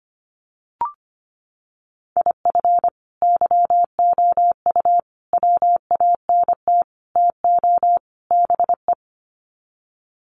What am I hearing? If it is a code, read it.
Morse code, heard as a keyed tone, about 25 wpm, 700 Hz. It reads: IF YOU WANT TO BE
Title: Sets of 5 Words Encore 25wpm